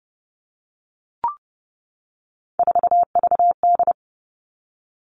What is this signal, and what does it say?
Morse code, heard as a keyed tone, about 30 wpm, 700 Hz. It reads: 4VD
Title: All Cut Numbers - 3-Digit 30wpm